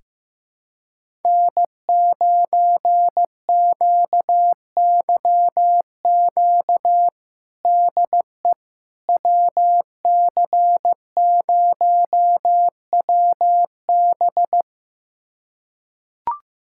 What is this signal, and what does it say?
Morse code, heard as a keyed tone, about 15 wpm, 700 Hz. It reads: N9QYQ DE WC0WB
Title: USA Callsign DE Callsign - 5 Characters - 15wpm